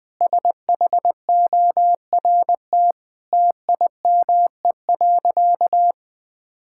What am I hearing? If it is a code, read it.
Morse code, heard as a keyed tone, about 20 wpm, 700 Hz. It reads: SHORT TIME.
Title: Sentences from Top 1600 Words 20wpm